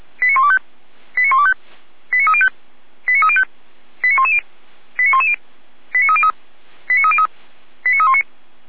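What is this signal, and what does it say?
Five-tone selective call (ZVEI), heard as quick runs of five tones, tones of about 70 ms. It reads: 89125 89125 89305 89305 89110 89110 89303 89303 89319
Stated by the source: ZVEI five-tone selective call received by SDR.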